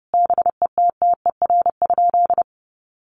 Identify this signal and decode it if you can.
Morse code, heard as a keyed tone, about 30 wpm, 700 Hz. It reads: BETTER?
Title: Sentences from Top 1200 Words 30wpm